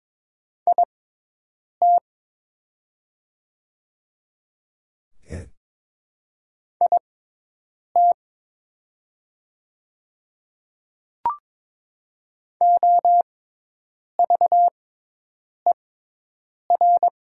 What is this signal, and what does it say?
Morse code, heard as a keyed tone, about 22 wpm, spaced as for 8 wpm, 700 Hz. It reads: IT IT OVER
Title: 6x Character Spacing - Top 100 Words - 22wpm